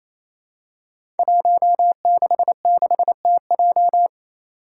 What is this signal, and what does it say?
Morse code, heard as a keyed tone, about 28 wpm, 700 Hz. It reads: 166TJ
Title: Five Challenging Letter-Number Combinations 28wpm